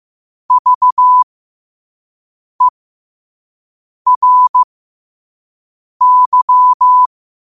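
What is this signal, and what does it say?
Morse code, heard as a keyed tone, about 15 wpm, spaced as for 5 wpm, 1000 Hz. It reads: VERY
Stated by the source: Morse code sent at 5 words per minute.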